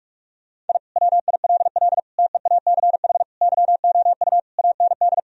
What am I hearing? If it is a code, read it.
Morse code, heard as a keyed tone, about 41 wpm, 700 Hz. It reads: I WILL TEACH YOU AND
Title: Sentences from Top 1000 Words - Review Part 3 - 45wpm